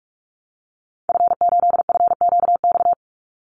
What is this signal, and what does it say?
Morse code, heard as a keyed tone, about 44 wpm, 700 Hz. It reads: F8FQX
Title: Call Signs Repeated 3x - 45wpm